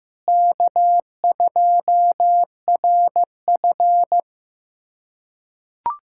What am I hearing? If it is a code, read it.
Morse code, heard as a keyed tone, about 15 wpm, 700 Hz. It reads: K2RF
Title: USA Callsign DE Callsign - 4 Characters - 15wpm